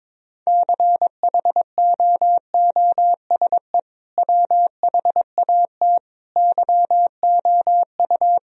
Morse code, heard as a keyed tone, about 22 wpm, 700 Hz. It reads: CHOOSE WHAT YOU
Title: Sentences from Top 1000 Words - Review Part 4 - 22wpm